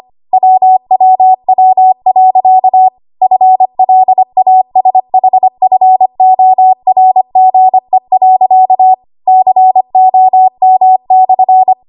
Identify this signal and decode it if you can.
Morse code, heard as a keyed tone, about 25 wpm, 750 Hz. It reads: WWW.FLASHFORGE.COM/
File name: Archive / 25 WPM / 25WPM